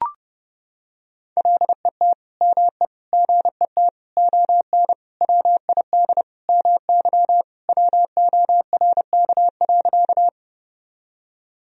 Morse code, heard as a keyed tone, about 30 wpm, 700 Hz. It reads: LET ME GET ON WID MY WORK.
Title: Sentences from Top 300 Words and Abbreviations 30wpm